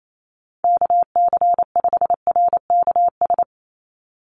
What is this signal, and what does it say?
Morse code, heard as a keyed tone, about 28 wpm, 700 Hz. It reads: KC5RKS
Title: USA Call Signs 28wpm